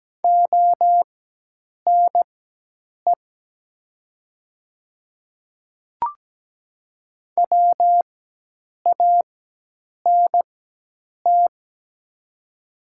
Morse code, heard as a keyed tone, about 17 wpm, spaced as for 8 wpm, 700 Hz. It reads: ONE WANT
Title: 4x Character Spacing - Top 200 Words - 17wpm